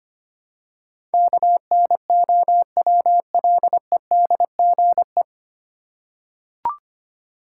Morse code, heard as a keyed tone, about 25 wpm, 700 Hz. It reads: KNOWLEDGE
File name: practice / 25wpm / 9 Letter Words - First 3 Letters Given - 25wpm